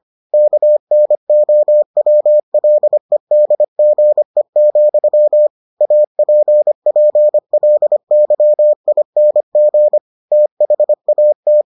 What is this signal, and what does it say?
Morse code, heard as a keyed tone, about 25 wpm, 600 Hz. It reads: KNOWLEDGE, APPLYING THAT